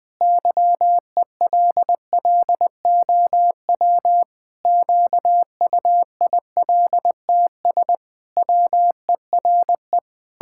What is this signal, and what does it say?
Morse code, heard as a keyed tone, about 20 wpm, 700 Hz. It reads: YELLOW QUILTS WERE
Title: Pangram Challenge v2 - 20wpm